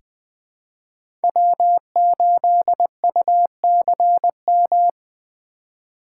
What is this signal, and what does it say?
Morse code, heard as a keyed tone, about 20 wpm, 700 Hz. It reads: W8UCM
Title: USA Call Signs - 5 Characters - 4x Character Spacing - 20wpm